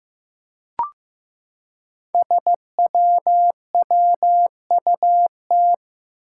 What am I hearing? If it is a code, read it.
Morse code, heard as a keyed tone, about 15 wpm, 700 Hz. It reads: SWWUT
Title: Five Challenging Letter-Number Combinations 15wpm